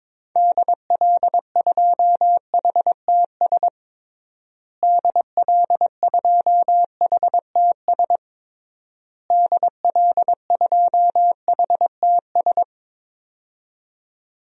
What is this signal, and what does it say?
Morse code, heard as a keyed tone, about 22 wpm, 700 Hz. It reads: DL2HTS DL2HTS DL2HTS
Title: DXCC-100 Call Signs - 6 Characters - Repeated 3x - 22wpm